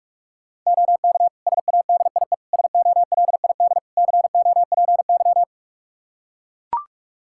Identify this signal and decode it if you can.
Morse code, heard as a keyed tone, about 41 wpm, 700 Hz. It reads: OK SADIE SOLID COPY
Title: Practice QSO - Exchange 3 - More Info - 45wpm